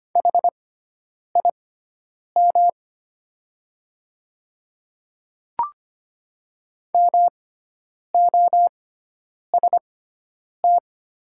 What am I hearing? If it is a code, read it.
Morse code, heard as a keyed tone, about 25 wpm, spaced as for 9 wpm, 700 Hz. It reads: HIM MOST
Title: 6x Character Spacing - Top 100 Words - 25wpm